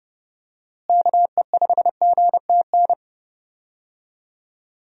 Morse code, heard as a keyed tone, about 32 wpm, 700 Hz. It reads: KE5GTN
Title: USA Call Signs - 6 Characters - Repeated 2x - 30wpm